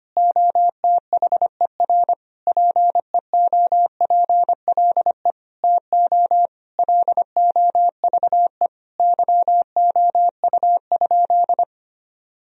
Morse code, heard as a keyed tone, about 25 wpm, 700 Hz. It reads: OTHER PEOPLE TO LOVE YOU?
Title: Sentences from Top 1000 Words - Review Part 3 - 25wpm